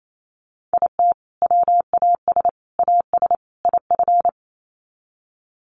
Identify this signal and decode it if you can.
Morse code, heard as a keyed tone, about 28 wpm, 700 Hz. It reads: IT WAS AS IF
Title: Sets of 4 Words 28wpm